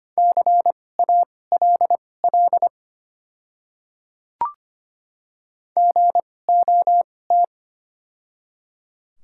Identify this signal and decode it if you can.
Morse code, heard as a keyed tone, about 25 wpm, spaced as for 18 wpm, 700 Hz. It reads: CALL GOT
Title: 2x Character Spacing - Top 300 Words - 25wpm